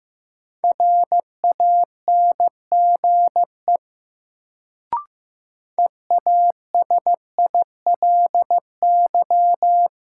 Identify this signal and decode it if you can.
Morse code, heard as a keyed tone, about 15 wpm, 700 Hz. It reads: RANGE EASILY
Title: Sentences from Top 900 Words - New Word Drill - 15wpm